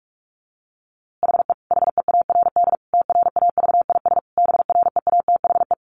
Morse code, heard as a keyed tone, about 45 wpm, 700 Hz. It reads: HE HEARD TRAVIS BREATHE
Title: A Sound Of Thunder 45wpm